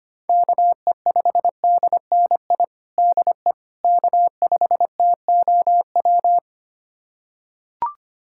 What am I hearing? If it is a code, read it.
Morse code, heard as a keyed tone, about 25 wpm, 700 Hz. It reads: KE5DNI DE K5TOW
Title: USA Callsign DE Callsign 25wpm